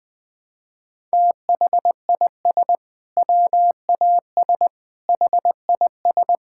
Morse code, heard as a keyed tone, about 21 wpm, 700 Hz. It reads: THIS WAS HIS
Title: Sentences from Top 1000 Words - Review Part 6 - 20wpm